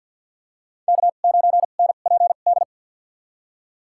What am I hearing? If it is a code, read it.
Morse code, heard as a keyed tone, about 51 wpm, spaced as for 36 wpm, 700 Hz. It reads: K9NPD
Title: USA Call Signs - 5 Characters - 2x Character Spacing - 50wpm